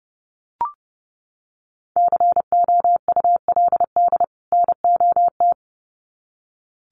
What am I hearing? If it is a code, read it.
Morse code, heard as a keyed tone, about 30 wpm, 700 Hz. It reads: COULD NOT
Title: Sets of 2 Words 30wpm